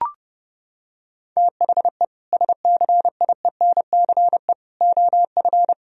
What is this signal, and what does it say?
Morse code, heard as a keyed tone, about 32 wpm, 700 Hz. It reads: THE SCIENCE OF
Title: Sentences from Top 1000 Words - Review Part 2 - 30wpm